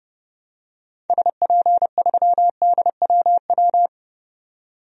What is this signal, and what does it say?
Morse code, heard as a keyed tone, about 30 wpm, 700 Hz. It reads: SP3DWW
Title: DXCC-100 Call Signs - 6 Characters - 30wpm